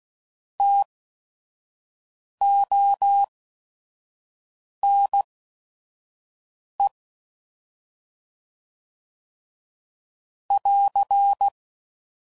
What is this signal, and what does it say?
Morse code, heard as a keyed tone, about 16 wpm, spaced as for 5 wpm, 800 Hz. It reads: TONE <AR>
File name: Prefixes / By First Letter / Ts